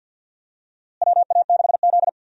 Morse code, heard as a keyed tone, about 50 wpm, 700 Hz. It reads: WA6Z
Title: USA Call Signs - 4 Characters - Repeated 3x - 50wpm